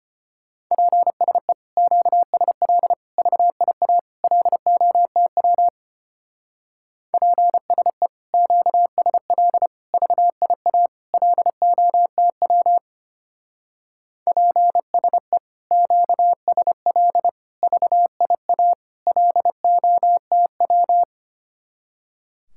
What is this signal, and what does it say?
Morse code, heard as a keyed tone, about 32 wpm, 700 Hz. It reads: PSE QSL VIA LOTW PSE QSL VIA LOTW PSE QSL VIA LOTW